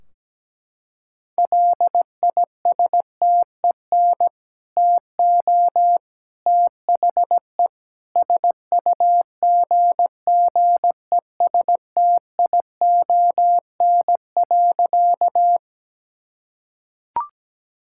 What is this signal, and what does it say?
Morse code, heard as a keyed tone, about 17 wpm, 700 Hz. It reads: LISTEN TO THE SUGGESTION.
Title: Sentences from Top 1600 Words 17wpm